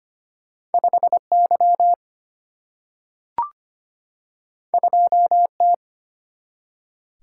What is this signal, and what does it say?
Morse code, heard as a keyed tone, about 25 wpm, 700 Hz. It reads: 5 Y 2 T
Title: Single-Letter Single-Number 25wpm